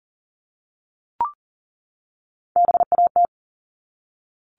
Morse code, heard as a keyed tone, about 40 wpm, 700 Hz. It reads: BAT
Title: Top 1251-1500 Words 40wpm